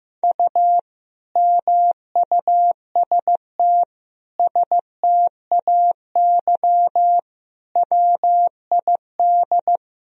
Morse code, heard as a keyed tone, about 15 wpm, 700 Hz. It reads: U MUST STAY WID
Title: Sentences from Top 500 Words and Abbreviations 15wpm